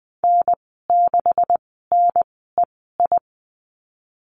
Morse code, heard as a keyed tone, about 20 wpm, spaced as for 15 wpm, 700 Hz. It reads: N6NEI
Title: 140 Lesson 23g - Letter f - Call Signs 20wpm